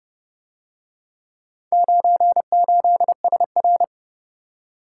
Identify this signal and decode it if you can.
Morse code, heard as a keyed tone, about 30 wpm, 700 Hz. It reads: 98SR